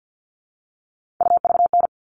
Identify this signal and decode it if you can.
Morse code, heard as a keyed tone, about 48 wpm, 700 Hz. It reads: UVN
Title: All Cut Numbers - 3-Digit 50wpm